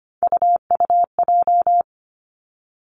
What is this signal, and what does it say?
Morse code, heard as a keyed tone, about 25 wpm, 700 Hz. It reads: UUJ
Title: Three Challenging Letter-Number Combinations 25wpm